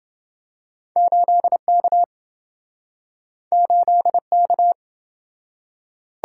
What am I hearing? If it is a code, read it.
Morse code, heard as a keyed tone, about 28 wpm, 700 Hz. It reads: 8K 8K